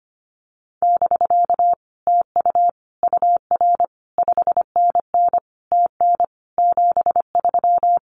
Morse code, heard as a keyed tone, about 25 wpm, 700 Hz. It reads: <BK> TU UR 5NN TN 73
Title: Common QSO Phrases Encore 25wpm